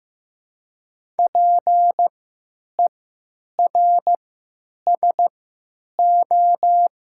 Morse code, heard as a keyed tone, about 15 wpm, 700 Hz. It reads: P E R S O
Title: 3x Character Spacing - Top 500 Words - 15wpm